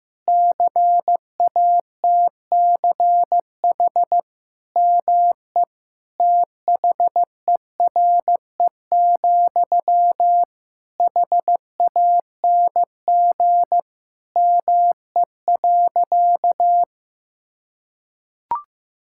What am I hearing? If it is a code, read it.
Morse code, heard as a keyed tone, about 15 wpm, 700 Hz. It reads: CATCH ME THERE, HANG ME.
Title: Sentences from Top 1600 Words 15wpm